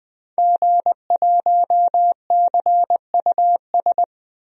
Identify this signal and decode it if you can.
Morse code, heard as a keyed tone, about 20 wpm, 700 Hz. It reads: G1CUS
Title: DXCC-100 Call Signs - 5 Characters - 20wpm